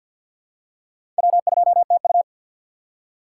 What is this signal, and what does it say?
Morse code, heard as a keyed tone, about 48 wpm, 700 Hz. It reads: W2TU
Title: USA Call Signs - 4 Characters - 6x Character Spacing - 50wpm